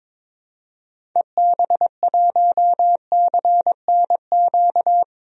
Morse code, heard as a keyed tone, about 22 wpm, 700 Hz. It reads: EB1CNQ
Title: DXCC-100 Call Signs - 6 Characters - 22wpm